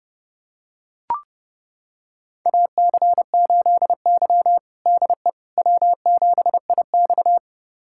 Morse code, heard as a keyed tone, about 30 wpm, 700 Hz. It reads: AC8Y DE W7IX
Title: USA Callsign DE Callsign - 4 Characters - 30wpm